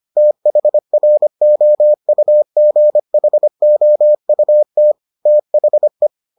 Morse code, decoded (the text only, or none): THROUGHOUT THE